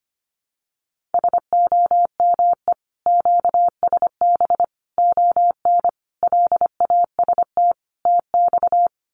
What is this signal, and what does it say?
Morse code, heard as a keyed tone, about 25 wpm, 700 Hz. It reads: SOME QSB ON LAST TX
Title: Common QSO Phrases 25wpm